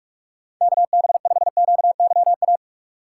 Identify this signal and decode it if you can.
Morse code, heard as a keyed tone, about 44 wpm, 700 Hz. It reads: KB5QYA
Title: USA Call Signs - 6 Characters - 2x Character Spacing - 45wpm